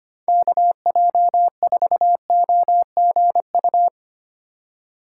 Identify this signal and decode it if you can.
Morse code, heard as a keyed tone, about 32 wpm, 700 Hz. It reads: KJ4OGU